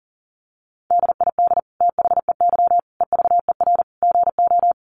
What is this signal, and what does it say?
Morse code, heard as a keyed tone, about 40 wpm, 700 Hz. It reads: DID THEY EVER GO
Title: Sentences from Top 400 Words 40wpm